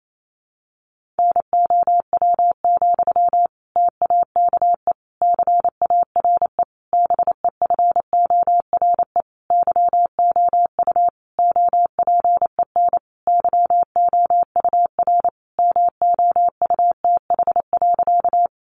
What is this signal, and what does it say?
Morse code, heard as a keyed tone, about 28 wpm, 700 Hz. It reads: NOW, TAKE CARE BEFORE YOU OPEN YOUR MOUTH.
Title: Sentences from Top 1500 Words 28wpm